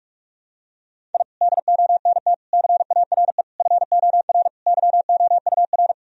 Morse code, heard as a keyed tone, about 46 wpm, 700 Hz. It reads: I DONT CARE FOR YOUR
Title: Sentences from Top 1000 Words - Review Part 3 - 45wpm